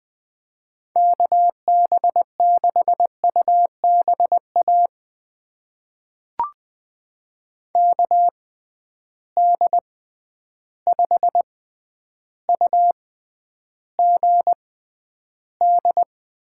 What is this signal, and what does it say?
Morse code, heard as a keyed tone, about 20 wpm, spaced as for 10 wpm, 700 Hz. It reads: KB6UBA KD5UGD
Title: USA Call Signs - 6x Character Spacing - 20wpm